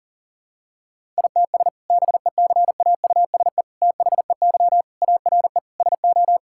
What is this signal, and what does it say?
Morse code, heard as a keyed tone, about 40 wpm, 700 Hz. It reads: ITS BECAUSE THEY ARE SO
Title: Sentences from Top 1000 Words - Review Part 6 - 40wpm